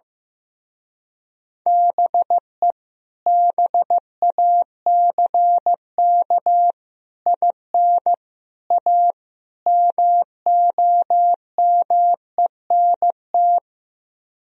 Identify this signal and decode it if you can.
Morse code, heard as a keyed tone, about 15 wpm, 700 Hz. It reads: BE BACK IN A MOMENT
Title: Speed-Racing - Sets of 5 Words 15wpm